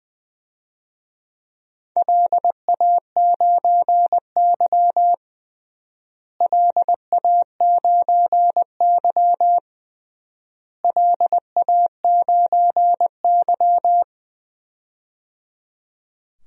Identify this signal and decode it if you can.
Morse code, heard as a keyed tone, about 20 wpm, 700 Hz. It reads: LA9Y LA9Y LA9Y
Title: DXCC-100 Call Signs - 4 Characters - Repeated 3x - 20wpm